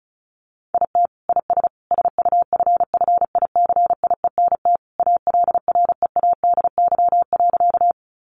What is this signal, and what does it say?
Morse code, heard as a keyed tone, about 34 wpm, 700 Hz. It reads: IT IS SUFFICIENT ALREADY.
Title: Sentences from Top 1400 Words 35wpm